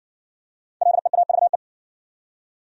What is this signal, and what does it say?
Morse code, heard as a keyed tone, about 60 wpm, 700 Hz. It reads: LEAVE